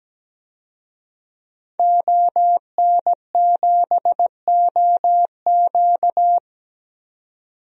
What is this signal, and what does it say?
Morse code, heard as a keyed tone, about 17 wpm, 700 Hz. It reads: ON7OQ
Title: DXCC-100 Call Signs - 5 Characters - 17wpm